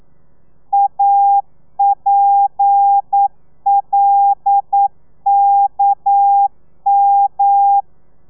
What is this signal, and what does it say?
Morse code, heard as a keyed tone, about 9 wpm, 800 Hz. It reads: APLKM
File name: telegrafia / i2tfj / Lez11 40